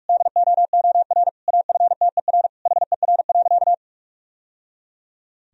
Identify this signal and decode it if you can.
Morse code, heard as a keyed tone, about 46 wpm, 700 Hz. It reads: DOOR AFTER HER.
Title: Sentences from Top 1000 Words - Review Part 1 - 45wpm